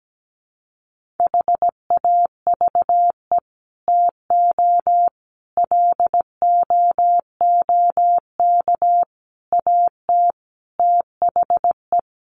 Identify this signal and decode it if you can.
Morse code, heard as a keyed tone, about 17 wpm, 700 Hz. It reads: HAVE TO LOOK AT THE
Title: Sets of 5 Words Encore 17wpm